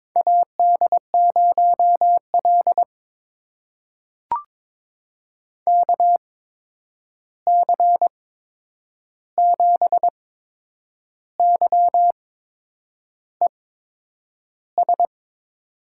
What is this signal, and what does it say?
Morse code, heard as a keyed tone, about 22 wpm, spaced as for 8 wpm, 700 Hz. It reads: AD0L KC7YES
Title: USA Call Signs - 8x Character Spacing - 22wpm